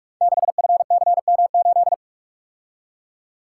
Morse code, heard as a keyed tone, about 44 wpm, 700 Hz. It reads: 6FCM8
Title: Code Groups - 5 Letter:Numbers - 45wpm